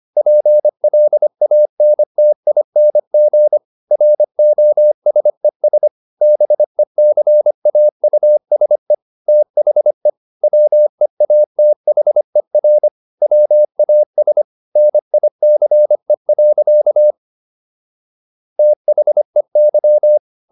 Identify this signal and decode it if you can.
Morse code, heard as a keyed tone, about 25 wpm, 600 Hz. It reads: PLANTING ROSES BECAUSE THE WEATHER WAS NICE. THEY